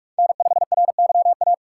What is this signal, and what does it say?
Morse code, heard as a keyed tone, about 42 wpm, 700 Hz. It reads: N5RYA
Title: USA Call Signs - 5 Characters - 2x Character Spacing - 45wpm